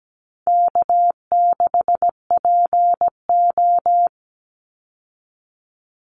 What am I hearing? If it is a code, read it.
Morse code, heard as a keyed tone, about 17 wpm, 700 Hz. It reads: K6PO
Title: Call Signs Repeated 3x - 17wpm